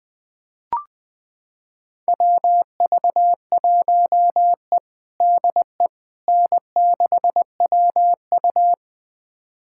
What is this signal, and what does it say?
Morse code, heard as a keyed tone, about 20 wpm, 700 Hz. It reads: WV1E DE N6WU
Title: USA Callsign DE Callsign - 4 Characters - 20wpm